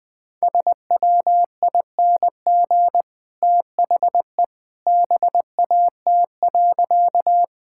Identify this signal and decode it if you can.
Morse code, heard as a keyed tone, about 20 wpm, 700 Hz. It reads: SWING THE BAT.
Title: Easy Sentences with Hints - 20wpm